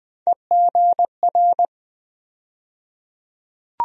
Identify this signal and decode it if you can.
Morse code, heard as a keyed tone, about 20 wpm, 700 Hz. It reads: EGR
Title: USA Call Sign Postfixes - 3 Letters 20wpm